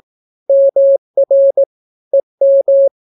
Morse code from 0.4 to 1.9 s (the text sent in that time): MR